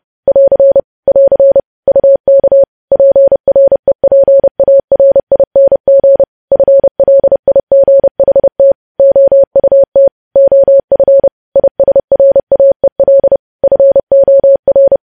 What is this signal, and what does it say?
Morse code, heard as a keyed tone, about 30 wpm, 550 Hz. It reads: <AR> <AR> UK PREPARING FLIGHT OUT OF ISRAEL FOR
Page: news headlines in morse code at 30 wpm